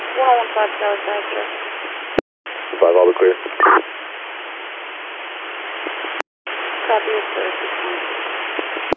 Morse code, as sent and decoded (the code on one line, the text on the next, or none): none